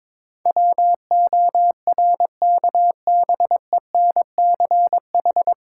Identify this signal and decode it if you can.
Morse code, heard as a keyed tone, about 21 wpm, 690 Hz. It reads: WORKBENCH
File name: Difficult Words with Hints - 22wpm